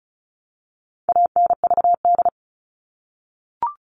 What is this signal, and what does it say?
Morse code, heard as a keyed tone, about 34 wpm, 700 Hz. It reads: ANVD
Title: All Cut Numbers - 4-Digit 35wpm